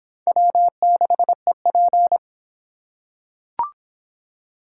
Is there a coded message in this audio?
Yes: Morse code from 0.3 to 2.2 s, heard as a keyed tone, about 22 wpm, 700 Hz. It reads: W6EP